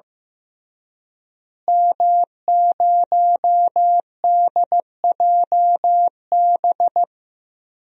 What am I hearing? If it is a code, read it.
Morse code, heard as a keyed tone, about 15 wpm, 700 Hz. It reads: M0DJB
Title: DXCC-100 Call Signs - 5 Characters - Repeated 3x - 15wpm